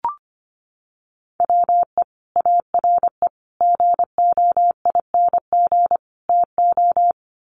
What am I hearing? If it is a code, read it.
Morse code, heard as a keyed tone, about 25 wpm, 700 Hz. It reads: WE ARE GOING TO
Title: Sets of 4 Words 25wpm